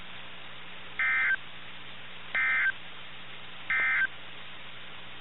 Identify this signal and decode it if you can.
Emergency Alert System (SAME) tones, heard as harsh buzzing bursts. It is heard from 0.9 to 4.1 s.
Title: NOAA Weather Radio - KIH61